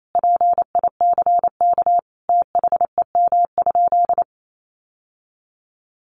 Morse code, heard as a keyed tone, about 28 wpm, 700 Hz. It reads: PICK THEM?
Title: Sentences from Top 1000 Words 28wpm